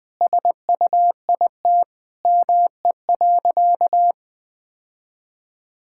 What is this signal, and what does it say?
Morse code, heard as a keyed tone, about 20 wpm, 700 Hz. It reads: SUIT ME.
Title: Sentences from Top 1600 Words 20wpm